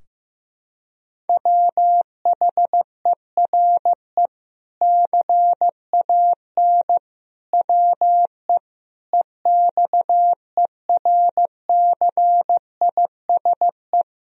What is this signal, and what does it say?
Morse code, heard as a keyed tone, about 15 wpm, 700 Hz. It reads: WHERE CAN WE EXERCISE
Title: Sentences from Top 1000 Words - Review Part 4 - 15wpm